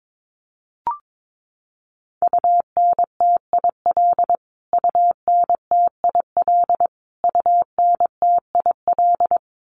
Morse code, heard as a keyed tone, about 22 wpm, 700 Hz. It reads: UNTIL UNTIL UNTIL
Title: Top 300 Words Repeated 22wpm